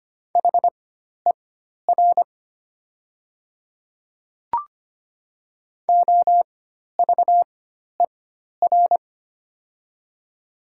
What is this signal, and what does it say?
Morse code, heard as a keyed tone, about 25 wpm, spaced as for 12 wpm, 700 Hz. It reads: HER OVER
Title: 4x Character Spacing - Top 100 Words - 25wpm